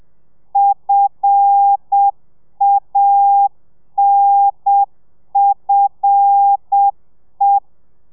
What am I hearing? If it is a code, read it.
Morse code, heard as a keyed tone, about 7 wpm, 800 Hz. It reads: FANFE
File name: telegrafia / i2tfj / Lez5 30